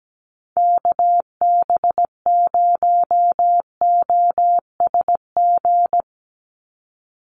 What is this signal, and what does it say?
Morse code, heard as a keyed tone, about 17 wpm, 700 Hz. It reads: KB0OSG
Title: USA Call Signs 17wpm